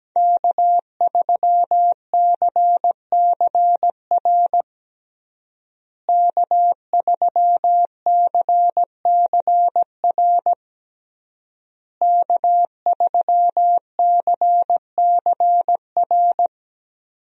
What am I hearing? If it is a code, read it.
Morse code, heard as a keyed tone, about 17 wpm, 700 Hz. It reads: K3CCR K3CCR K3CCR
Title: USA Call Signs - 5 Characters - Repeated 3x - 17wpm